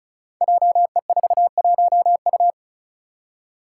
Morse code, heard as a keyed tone, about 36 wpm, 700 Hz. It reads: JE41U
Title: Five Challenging Letter-Number Combinations 35wpm